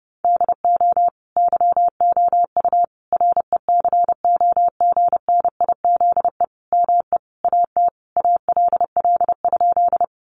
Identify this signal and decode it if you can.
Morse code, heard as a keyed tone, about 30 wpm, 700 Hz. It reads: DO YOU RECOGNIZE ME AT ALL?
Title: Sentences from Top 1400 Words 30wpm